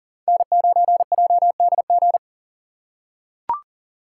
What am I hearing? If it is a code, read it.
Morse code, heard as a keyed tone, about 40 wpm, 700 Hz. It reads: N9JDG
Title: USA Call Signs - 5 Characters - 2x Character Spacing - 40wpm